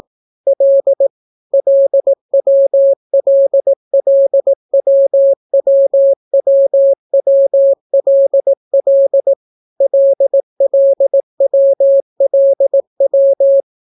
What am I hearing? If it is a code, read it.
Morse code, heard as a keyed tone, about 18 wpm, 550 Hz. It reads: L LWLLWWWWLL LLWLW